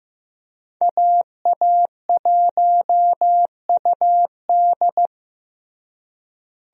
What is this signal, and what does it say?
Morse code, heard as a keyed tone, about 15 wpm, 700 Hz. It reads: AA1UD
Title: USA Call Signs - 5 Characters - Repeated 2x - 15wpm